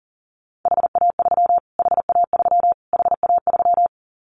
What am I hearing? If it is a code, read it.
Morse code, heard as a keyed tone, about 40 wpm, 700 Hz. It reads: HA3 HA3 HA3
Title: Callsign Prefixes for Top 25 DXCC - Repeated 3x - 40wpm